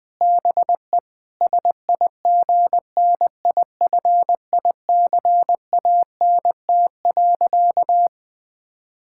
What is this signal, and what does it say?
Morse code, heard as a keyed tone, about 21 wpm, 700 Hz. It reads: BE SIGNIFICANT.
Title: Sentences from Top 1000 Words - Review Part 6 - 20wpm